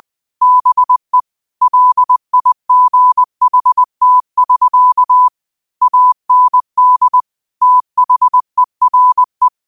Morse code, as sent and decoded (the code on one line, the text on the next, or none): -... / .-.. .. --. .... - ...-.- / .- -. -.. / - .... .-. .
BE LIGHT<SK> AND THERE